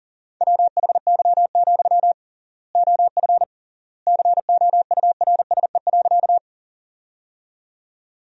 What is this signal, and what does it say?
Morse code, heard as a keyed tone, about 37 wpm, 700 Hz. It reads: WHY, OF COURSE.
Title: 3x Word-Spacing - Sentences from Top 500 Words - 40wpm